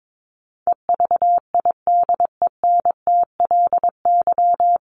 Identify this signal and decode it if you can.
Morse code, heard as a keyed tone, about 22 wpm, 700 Hz. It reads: EVIDENTLY
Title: LY Suffix - Top 100 Words with Base Words 22wpm